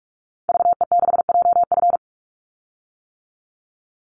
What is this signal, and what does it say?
Morse code, heard as a keyed tone, about 44 wpm, 700 Hz. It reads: VE6JF
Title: Call Signs - 45wpm